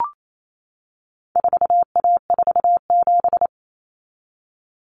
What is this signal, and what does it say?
Morse code, heard as a keyed tone, about 28 wpm, 700 Hz. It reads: 4A47
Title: Common Cut Numbers - 4-Digit 28wpm